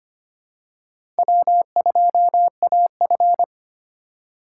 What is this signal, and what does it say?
Morse code, heard as a keyed tone, about 25 wpm, 700 Hz. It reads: W2AF
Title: USA Call Signs - 4 Characters - 2x Character Spacing - 25wpm